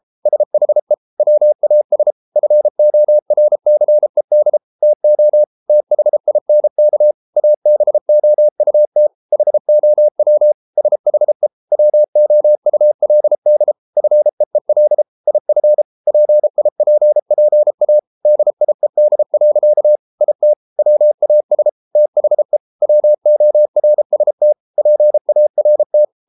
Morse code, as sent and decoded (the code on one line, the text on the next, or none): ... .... / .-- .- ... / ..-. --- .-. -.-. -.. / - --- / - .... .. -. -.- / .- -... --- ..- - / .... --- .-- / ... .... / .-- --- ..- .-.. -.. / ..-. . .-.. / .. ..-. / .--. .. .--. .--. .- / -.. .. -.. .-.-.- / .. - / .-- .- ... / - .... / .-- --- .-. ... - / .--. .- .-. -
SHE WAS FORCED TO THINK ABOUT HOW SHE WOULD FEEL IF PIPPA DIED. IT WAS THE WORST PART